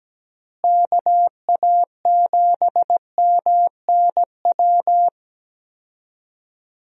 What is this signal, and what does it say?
Morse code, heard as a keyed tone, about 17 wpm, 700 Hz. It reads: KA7MNW
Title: USA Call Signs - 6 Characters - Repeated 3x - 17wpm